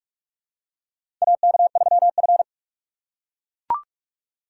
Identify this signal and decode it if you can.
Morse code, heard as a keyed tone, about 44 wpm, 700 Hz. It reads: AK3F
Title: USA Call Signs - 6x Character Spacing - 45wpm